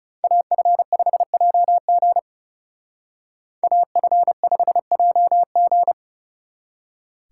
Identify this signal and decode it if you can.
Morse code, heard as a keyed tone, about 36 wpm, 700 Hz. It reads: AF5JG AF5JG
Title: Speed Racing - USA 2x2 Call Signs 30wpm